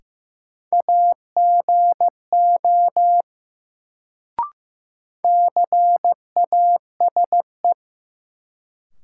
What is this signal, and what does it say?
Morse code, heard as a keyed tone, about 15 wpm, 700 Hz. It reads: AGO CASE
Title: Sets of 2 Words - New Word Drill - 15wpm